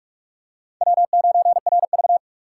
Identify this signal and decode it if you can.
Morse code, heard as a keyed tone, about 39 wpm, 700 Hz. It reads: W9RV